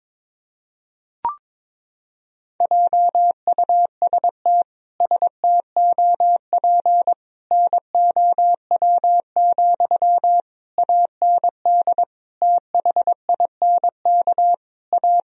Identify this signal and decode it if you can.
Morse code, heard as a keyed tone, about 22 wpm, 700 Hz. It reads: JUST STOP NOW, AND THINK A
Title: Sentences from Top 1000 Words - Review Part 1 - 22wpm